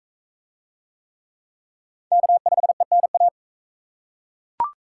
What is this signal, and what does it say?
Morse code, heard as a keyed tone, about 38 wpm, 700 Hz. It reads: K5ENA